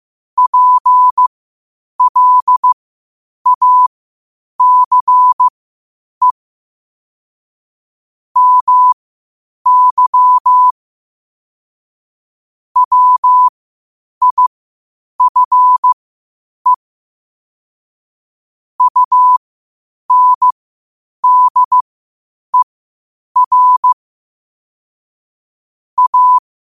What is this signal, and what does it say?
Morse code, heard as a keyed tone, about 15 wpm, spaced as for 8 wpm, 1000 Hz. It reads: PLACE MY WIFE UNDER A
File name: qotdfiles / 8wpm